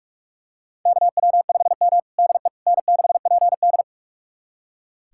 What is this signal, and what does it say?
Morse code, heard as a keyed tone, about 41 wpm, 700 Hz. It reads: KW5M DE N6PD
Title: USA Callsign DE Callsign - 4 Characters - 45wpm